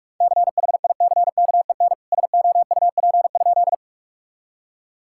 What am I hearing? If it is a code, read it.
Morse code, heard as a keyed tone, about 46 wpm, 700 Hz. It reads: CHICKEN SOUP?
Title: Sentences from Top 1000 Words - Review Part 6 - 45wpm